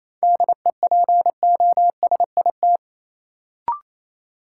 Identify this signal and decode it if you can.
Morse code, heard as a keyed tone, about 28 wpm, 700 Hz. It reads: DEPOSIT
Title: Sentences from Top 1300 Words - New Word Drill - 28wpm